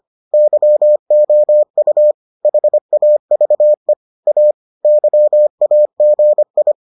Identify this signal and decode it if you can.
Morse code, heard as a keyed tone, about 25 wpm, 600 Hz. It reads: YOU HAVE A YAGI